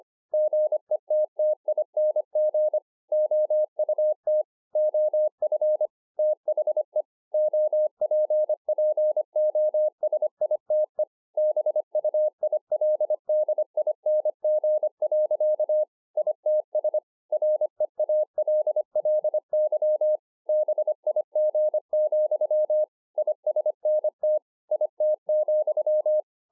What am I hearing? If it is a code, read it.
Morse code, heard as a keyed tone, about 25 wpm, 610 Hz. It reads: GETTING OUT OF THE OPPOSITE BUILDING. ITS REALLY BIG, ISNT IT,